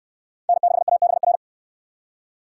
Morse code, heard as a keyed tone, about 69 wpm, 700 Hz. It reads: N6ABR